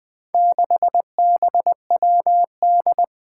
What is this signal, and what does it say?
Morse code, heard as a keyed tone, about 20 wpm, 700 Hz. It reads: 6BWD
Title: Four Challenging Letter-Number Combinations 20wpm